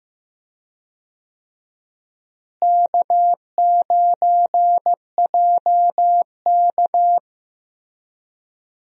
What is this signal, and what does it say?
Morse code, heard as a keyed tone, about 15 wpm, 700 Hz. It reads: K9JK
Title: USA Call Signs - 4 Characters - Repeated 2x - 15wpm